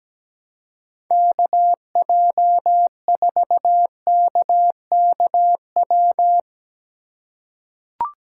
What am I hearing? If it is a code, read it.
Morse code, heard as a keyed tone, about 17 wpm, 700 Hz. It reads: KJ4KKW
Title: USA Call Signs - 2x Character Spacing - 17wpm